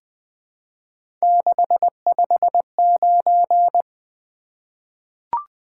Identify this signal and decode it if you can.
Morse code, heard as a keyed tone, about 20 wpm, 700 Hz. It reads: 659
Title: Three Letter-Number Combinations - 20wpm